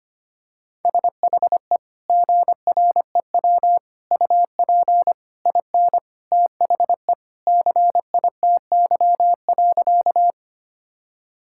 Morse code, heard as a keyed tone, about 25 wpm, 700 Hz. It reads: SHE GREW UP IN THE CITY.
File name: practice / 25wpm / Sentences with 2-Word Phrases - 25wpm